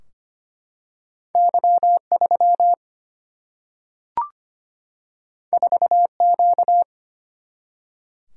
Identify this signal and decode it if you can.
Morse code, heard as a keyed tone, about 25 wpm, 700 Hz. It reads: Y 3 4 Q
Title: Single-Letter Single-Number 25wpm